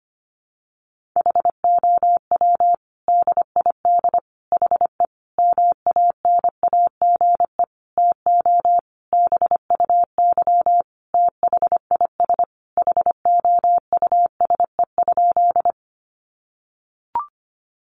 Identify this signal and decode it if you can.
Morse code, heard as a keyed tone, about 25 wpm, 700 Hz. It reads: HOW DID HE MANAGE TO BUY THIS HOUSE?
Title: Sentences from Top 1400 Words 25wpm